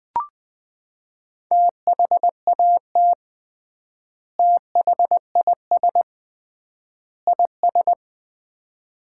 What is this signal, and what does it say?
Morse code, heard as a keyed tone, about 20 wpm, 700 Hz. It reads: THAT THIS IS
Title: 3x Word-Spacing - Sets of 3 Words - 20wpm